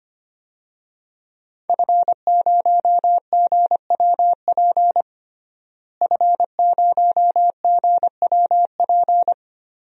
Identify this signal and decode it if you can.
Morse code, heard as a keyed tone, about 25 wpm, 700 Hz. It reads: F0GWP F0GWP
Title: DXCC-100 Call Signs - 5 Characters - Repeated 2x - 25wpm